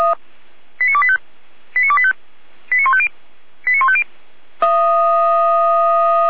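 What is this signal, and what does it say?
Five-tone selective call (ZVEI), heard as quick runs of five tones, tones of about 70 ms. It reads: 89285 89285 89150 89150
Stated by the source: ZVEI five-tone fire brigade alert for code 89508, received by SDR, from Feuerwehr Orlamünde.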